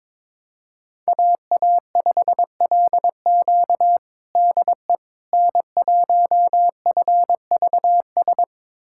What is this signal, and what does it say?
Morse code, heard as a keyed tone, about 21 wpm, 700 Hz. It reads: AA5LQ DE N1FVS
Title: USA Callsign DE Callsign - 5 Characters - 22wpm